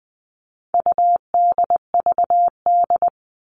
Morse code, heard as a keyed tone, about 20 wpm, 700 Hz. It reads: UDVD
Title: All Cut Numbers - 4-Digit 20wpm